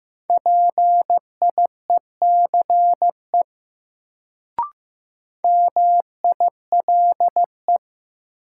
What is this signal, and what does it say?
Morse code, heard as a keyed tone, about 15 wpm, 700 Hz. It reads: PIECE MILE
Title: Sentences from Top 400 Words - New Word Drill - 15wpm